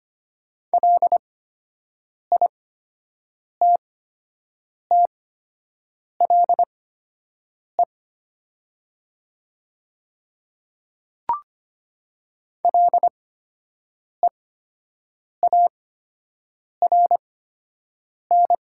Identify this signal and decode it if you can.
Morse code, heard as a keyed tone, about 25 wpm, spaced as for 7 wpm, 700 Hz. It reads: LITTLE LEARN
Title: 8x Character Spacing - Top 300 Words - 25wpm